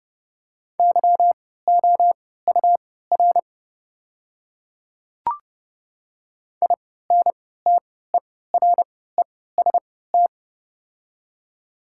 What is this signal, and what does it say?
Morse code, heard as a keyed tone, about 30 wpm, spaced as for 17 wpm, 700 Hz. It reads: YOUR INTEREST
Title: 3x Character Spacing - Top 500 Words - 30wpm